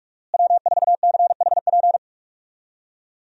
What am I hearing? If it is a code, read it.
Morse code, heard as a keyed tone, about 41 wpm, 700 Hz. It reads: W4CHP
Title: USA Call Signs - 5 Characters - Repeated 2x - 45wpm